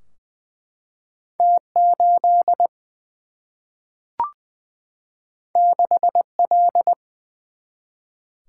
Morse code, heard as a keyed tone, about 20 wpm, 700 Hz. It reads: T 8 6 L
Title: Single-Letter Single-Number 20wpm